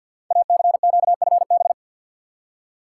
Morse code, heard as a keyed tone, about 48 wpm, 700 Hz. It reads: ACQFB
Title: Code Groups - 5 Letters - 50wpm